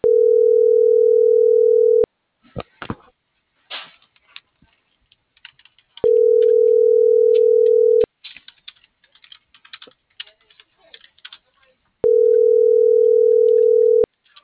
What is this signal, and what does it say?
Ringback tone (North America), 440 + 480 Hz.